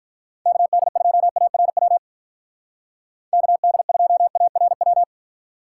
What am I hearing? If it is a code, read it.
Morse code, heard as a keyed tone, about 45 wpm, 700 Hz. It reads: KD2ARW KD2ARW